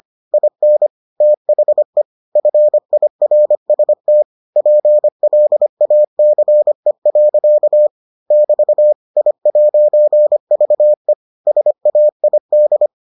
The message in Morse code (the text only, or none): IN THE FIRST PLACE. <BT> I'VE SAID